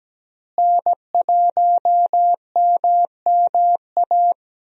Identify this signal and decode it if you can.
Morse code, heard as a keyed tone, about 17 wpm, 700 Hz. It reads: N1MMA
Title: USA Call Signs - 5 Characters - Repeated 2x - 17wpm